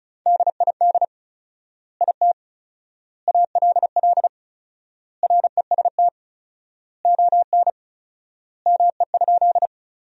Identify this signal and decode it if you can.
Morse code, heard as a keyed tone, about 34 wpm, 700 Hz. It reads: DID IT ALL REST ON ME?
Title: 4x Word-Spacing - Sentences from Top 500 Words - 35wpm